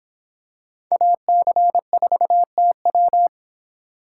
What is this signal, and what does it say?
Morse code, heard as a keyed tone, about 25 wpm, 700 Hz. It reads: AC4TW